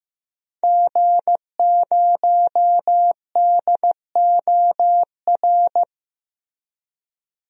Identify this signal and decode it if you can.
Morse code, heard as a keyed tone, about 15 wpm, 700 Hz. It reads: G0DOR
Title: DXCC-100 Call Signs - 5 Characters - Repeated 2x - 15wpm